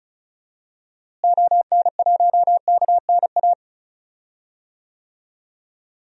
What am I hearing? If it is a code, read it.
Morse code, heard as a keyed tone, about 34 wpm, 700 Hz. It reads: ON1KNA
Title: DXCC-100 Call Signs - 6 Characters - Repeated 3x - 35wpm